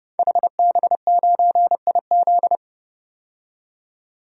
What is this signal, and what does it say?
Morse code, heard as a keyed tone, about 30 wpm, 700 Hz. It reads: HB9IZ
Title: DXCC-100 Call Signs - 5 Characters - 30wpm